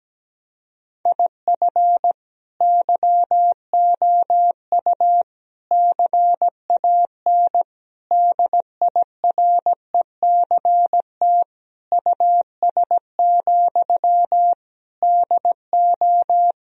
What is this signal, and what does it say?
Morse code, heard as a keyed tone, about 17 wpm, 700 Hz. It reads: IF YOU CAN DIRECT US, DO
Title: Sentences from Top 1000 Words - Review Part 2 - 17wpm